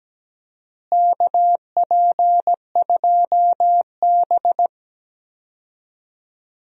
Morse code, heard as a keyed tone, about 17 wpm, 700 Hz. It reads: KP2B
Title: DXCC-100 Call Signs - 4 Characters - Repeated 3x - 17wpm